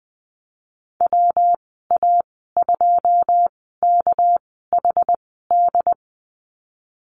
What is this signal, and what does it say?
Morse code, heard as a keyed tone, about 20 wpm, spaced as for 14 wpm, 700 Hz. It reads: WA2KHD
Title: 213 Lesson 34g - Number 8 - Call Signs 20wpm